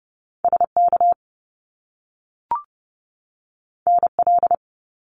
Morse code, heard as a keyed tone, about 30 wpm, 700 Hz. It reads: SK NL
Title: Canadian Provinces and Territories 30wpm